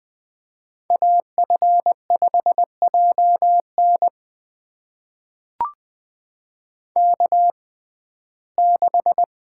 Morse code, heard as a keyed tone, about 20 wpm, spaced as for 14 wpm, 700 Hz. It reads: AF5JN K6
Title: USA Call Signs - 5 Characters - 6x Character Spacing - 20wpm